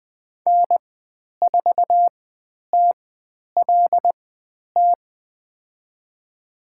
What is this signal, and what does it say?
Morse code, heard as a keyed tone, about 20 wpm, spaced as for 10 wpm, 700 Hz. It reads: N4TLT
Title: 102 Lesson 17g - Letter u - Call Signs 20wpm